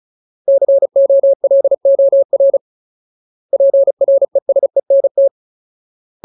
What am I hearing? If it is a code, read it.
Morse code, heard as a keyed tone, about 34 wpm, 550 Hz. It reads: COLOR PRESENT
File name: Ultimate Word List / 35WPM Ultimate